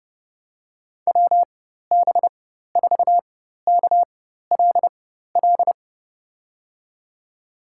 Morse code, heard as a keyed tone, about 31 wpm, spaced as for 14 wpm, 700 Hz. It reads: WB4KLL
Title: USA Call Signs - 6 Characters - 4x Character Spacing - 30wpm